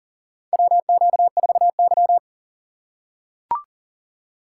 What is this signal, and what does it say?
Morse code, heard as a keyed tone, about 40 wpm, 700 Hz. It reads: WQ4Y
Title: USA Call Signs - 3x Character Spacing - 40wpm